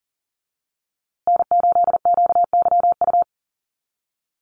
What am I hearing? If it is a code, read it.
Morse code, heard as a keyed tone, about 40 wpm, 700 Hz. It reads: N8QYU
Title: USA Call Signs Repeated 2x 40wpm